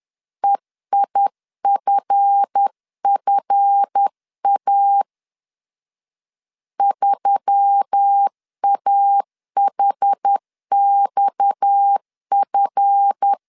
Morse code, heard as a keyed tone, about 10 wpm, 800 Hz. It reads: EIFFA 3AHXF
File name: LEZIONI / 7 GF9Z / 7 GF 10wpm 2514